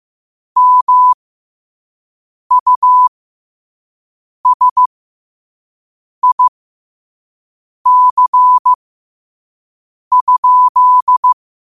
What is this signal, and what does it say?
Morse code, heard as a keyed tone, about 15 wpm, spaced as for 5 wpm, 1000 Hz. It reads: MUSIC?
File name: 5wpm